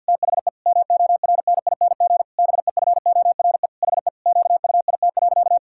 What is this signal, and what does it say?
Morse code, heard as a keyed tone, about 50 wpm, 700 Hz. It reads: THE MORNING BEFORE HE QUIT.
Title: Pangram Challenge v2 - 50wpm